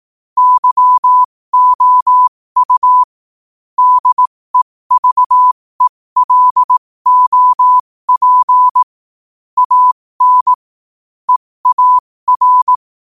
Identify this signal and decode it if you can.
Morse code, heard as a keyed tone, about 18 wpm, spaced as for 15 wpm, 1000 Hz. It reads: YOU DEVELOP AN EAR
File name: qotdfiles / 15wpm